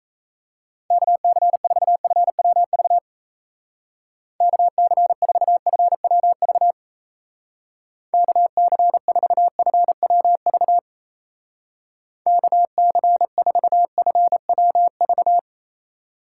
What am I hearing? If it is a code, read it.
Morse code, heard as a keyed tone, about 34 wpm, 700 Hz. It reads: KC4FWV KC4FWV KC4FWV KC4FWV
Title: Speed Racing - USA Call Signs 28wpm